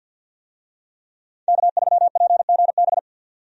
Morse code, heard as a keyed tone, about 49 wpm, 700 Hz. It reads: K3PGB
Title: USA Call Signs - 5 Characters - 50wpm